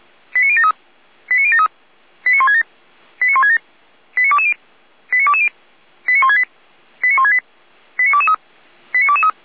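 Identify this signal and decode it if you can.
Five-tone selective call (ZVEI), heard as quick runs of five tones, tones of about 70 ms. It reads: 89073 89073 89167 89167 89220 89220 89168 89168 89223 89223
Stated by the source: ZVEI five-tone call received by SDR.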